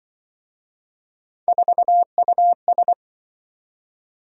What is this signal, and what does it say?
Morse code, heard as a keyed tone, about 25 wpm, 700 Hz. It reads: 4US
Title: Speed-Racing - Three Letter-Number Combinations - 20wpm